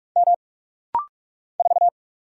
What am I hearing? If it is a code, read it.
Morse code, heard as a keyed tone, about 44 wpm, 690 Hz. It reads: M 4
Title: Single Letter-Number - Rapid-Fire 45wpm